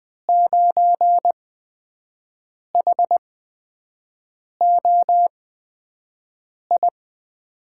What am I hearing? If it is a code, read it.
Morse code, heard as a keyed tone, about 20 wpm, spaced as for 5 wpm, 700 Hz. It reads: 9HOI
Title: USA Call Signs - 5 Characters - 8x Character Spacing - 20wpm